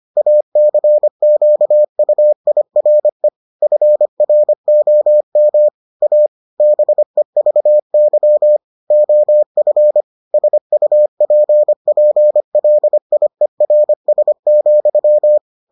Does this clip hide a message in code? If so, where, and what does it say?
Yes: Morse code from 0.2 to 15.4 s, heard as a keyed tone, about 25 wpm, 600 Hz. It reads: ACQUIRE FROM A BEVY OF SUPPLIERS,